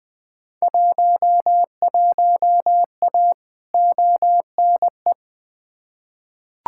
Morse code, heard as a keyed tone, about 20 wpm, 700 Hz. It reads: 11A ONE
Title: Field Day Exchanges Encore - 20wpm